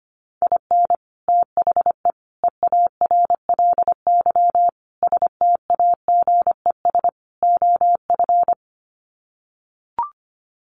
Morse code, heard as a keyed tone, about 25 wpm, 690 Hz. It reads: IN THE EARLY STAGES OF
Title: Sets of 5 Words Encore 25wpm